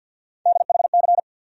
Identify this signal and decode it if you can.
Morse code, heard as a keyed tone, about 49 wpm, 700 Hz. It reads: DHC
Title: USA Call Sign Postfixes - 3 Letters 50wpm